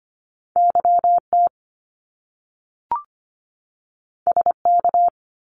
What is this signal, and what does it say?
Morse code, heard as a keyed tone, about 25 wpm, 700 Hz. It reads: YT SK
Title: Canadian Provinces and Territories 25wpm